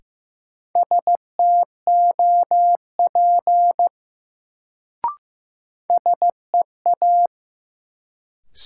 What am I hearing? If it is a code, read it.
Morse code, heard as a keyed tone, about 15 wpm, 700 Hz. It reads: STOP SEA
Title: Sentences from Top 300 Words - New Word Drill - 15wpm